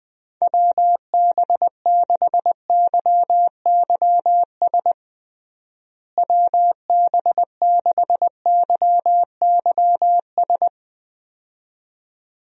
Morse code, heard as a keyed tone, about 21 wpm, 700 Hz. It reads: WB6YYS WB6YYS
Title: USA Call Signs - 6 Characters - Repeated 2x - 20wpm